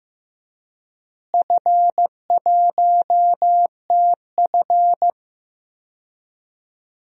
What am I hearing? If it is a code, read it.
Morse code, heard as a keyed tone, about 15 wpm, 700 Hz. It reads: F1TF
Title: DXCC-100 Call Signs - 4 Characters - Repeated 2x - 15wpm